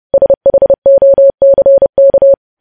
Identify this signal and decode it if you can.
Morse code, heard as a keyed tone, about 30 wpm, 550 Hz. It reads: SHOCK
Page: news headlines in morse code at 30 wpm